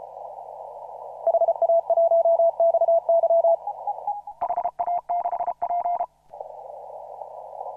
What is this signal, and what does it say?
Morse code, heard as a keyed tone, about 34 wpm, 690 Hz. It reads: HA1XY HA6P
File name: CW / RHOB CW 2017 HA6P 0001